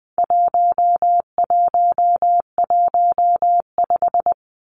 Morse code, heard as a keyed tone, about 20 wpm, 700 Hz. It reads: 1115
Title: Numbers - 4-Digit 20wpm